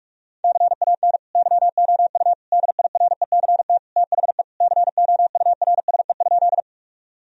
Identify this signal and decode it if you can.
Morse code, heard as a keyed tone, about 46 wpm, 700 Hz. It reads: CAN YOU DIRECT THE COURSE?
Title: Sentences from Top 1000 Words - Review Part 2 - 45wpm